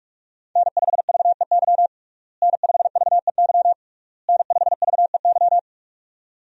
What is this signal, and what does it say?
Morse code, heard as a keyed tone, about 46 wpm, 700 Hz. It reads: N5VEY N5VEY N5VEY
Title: USA Call Signs - 5 Characters - Repeated 3x - 45wpm